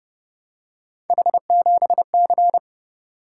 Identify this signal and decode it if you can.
Morse code, heard as a keyed tone, about 28 wpm, 700 Hz. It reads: H7C